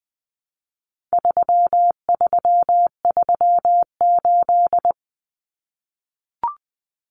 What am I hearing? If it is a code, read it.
Morse code, heard as a keyed tone, about 20 wpm, 700 Hz. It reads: 3338
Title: Common Cut Numbers - 4-Digit 20wpm